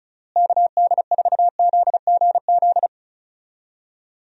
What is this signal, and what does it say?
Morse code, heard as a keyed tone, about 36 wpm, 700 Hz. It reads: KD4ZGZ
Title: USA Call Signs - 6 Characters - Repeated 3x - 35wpm